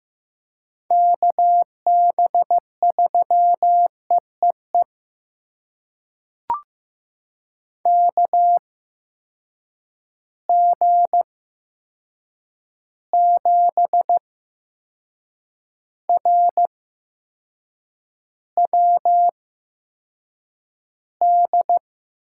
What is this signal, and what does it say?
Morse code, heard as a keyed tone, about 15 wpm, spaced as for 6 wpm, 700 Hz. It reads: KB3EEE KG7RWD
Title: USA Call Signs - 6 Characters - 8x Character Spacing - 15wpm